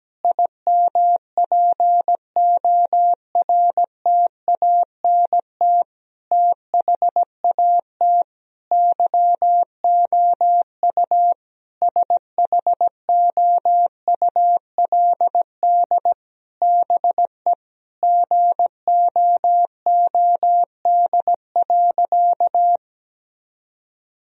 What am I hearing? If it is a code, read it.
Morse code, heard as a keyed tone, about 17 wpm, 700 Hz. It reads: IMPORTANT THAT YOU SHOULD BE GOOD.
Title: Sentences from Top 1000 Words - Review Part 1 - 17wpm